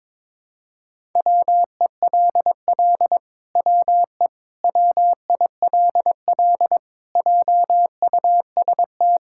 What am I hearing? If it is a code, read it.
Morse code, heard as a keyed tone, about 22 wpm, 700 Hz. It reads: WELL WE WILL JUST
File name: Sentences with 7-Word Phrases - 22wpm